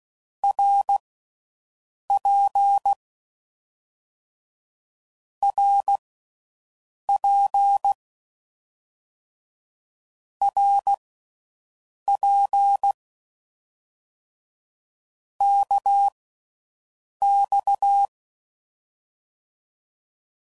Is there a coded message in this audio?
Yes: Morse code from 0.4 to 18.1 s, heard as a keyed tone, about 16 wpm, spaced as for 6 wpm, 800 Hz. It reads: RP RP RP KX